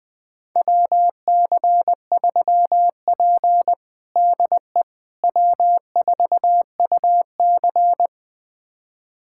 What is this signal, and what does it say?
Morse code, heard as a keyed tone, about 20 wpm, 700 Hz. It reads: WC3P DE W4UC
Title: USA Callsign DE Callsign - 4 Characters - 20wpm